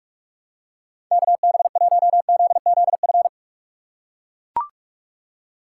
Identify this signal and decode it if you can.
Morse code, heard as a keyed tone, about 42 wpm, 700 Hz. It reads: KB1ZZF
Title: USA Call Signs - 6 Characters - 4x Character Spacing - 45wpm